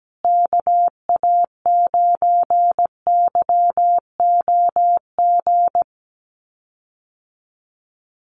Morse code, heard as a keyed tone, about 17 wpm, 700 Hz. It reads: KA9YOG
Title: USA Call Signs Repeated 2x 17wpm